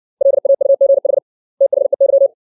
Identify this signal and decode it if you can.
Morse code, heard as a keyed tone, about 60 wpm, 550 Hz. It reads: LAUGH THEY
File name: 632 words / 632 Words 60WPM